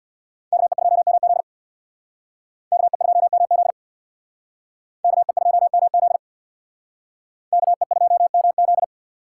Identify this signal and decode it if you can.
Morse code, heard as a keyed tone, about 61 wpm, 700 Hz. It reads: KE2MZ KE2MZ KE2MZ KE2MZ
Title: Speed Racing - USA 2x2 Call Signs 50wpm